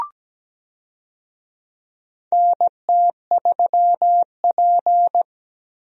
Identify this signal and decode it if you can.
Morse code, heard as a keyed tone, about 17 wpm, 700 Hz. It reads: NT3P
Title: USA Call Signs - 4 Characters - Repeated 3x - 17wpm